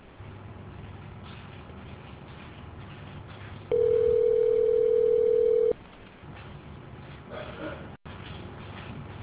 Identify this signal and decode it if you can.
Ringback tone (North America), 440 + 480 Hz.